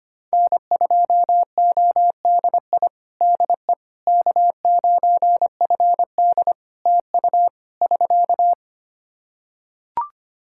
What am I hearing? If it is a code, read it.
Morse code, heard as a keyed tone, about 25 wpm, 700 Hz. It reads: N2ODI DE K9FD TU <SK>
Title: Practice QSO - Exchange 4 - Ending - 25wpm